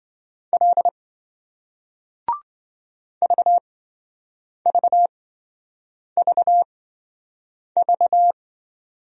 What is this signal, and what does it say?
Morse code, heard as a keyed tone, about 28 wpm, spaced as for 7 wpm, 700 Hz. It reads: L VVVV